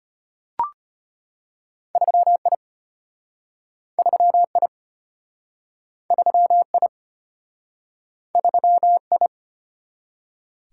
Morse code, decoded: 3I 3I 3I 3I

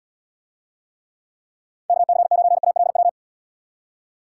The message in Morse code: KC9ALW

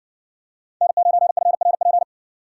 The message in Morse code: N9FRP

